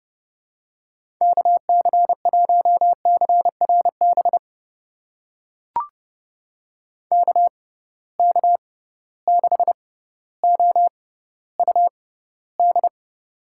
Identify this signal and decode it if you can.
Morse code, heard as a keyed tone, about 30 wpm, spaced as for 15 wpm, 700 Hz. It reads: KC1CRB KK6OUD
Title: USA Call Signs - 6 Characters - 6x Character Spacing - 30wpm